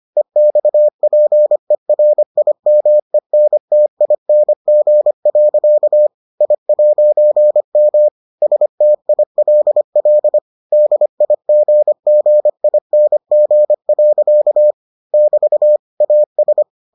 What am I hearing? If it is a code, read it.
Morse code, heard as a keyed tone, about 25 wpm, 600 Hz. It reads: EXPERIMENTING. I'M STILL DIGGING. <BT> AS